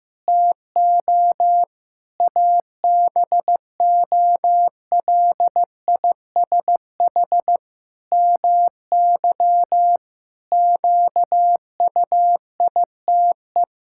Morse code, heard as a keyed tone, about 15 wpm, 700 Hz. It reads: TO ABOLISH MY QUITE
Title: Pangram Challenge v2 - 15wpm